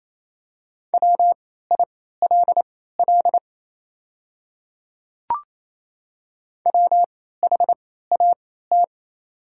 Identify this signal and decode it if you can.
Morse code, heard as a keyed tone, about 27 wpm, spaced as for 16 wpm, 700 Hz. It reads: WILL WHAT
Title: 3x Character Spacing - Top 100 Words - 28wpm